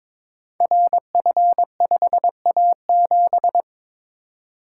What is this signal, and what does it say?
Morse code, heard as a keyed tone, about 22 wpm, 700 Hz. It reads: RF5A7
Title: Code Groups - 5 Letter:Numbers - 22wpm